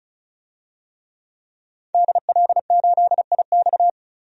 Morse code, heard as a keyed tone, about 36 wpm, 700 Hz. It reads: DL8IX
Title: DXCC-100 Call Signs - 5 Characters - 35wpm